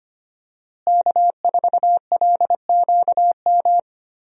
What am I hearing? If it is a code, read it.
Morse code, heard as a keyed tone, about 25 wpm, 700 Hz. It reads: K4LQM
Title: USA Call Signs - 5 Characters - Repeated 3x - 25wpm